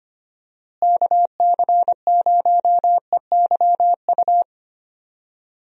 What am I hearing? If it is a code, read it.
Morse code, heard as a keyed tone, about 25 wpm, 700 Hz. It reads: KC0EYU
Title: USA Call Signs - 6 Characters - Repeated 3x - 25wpm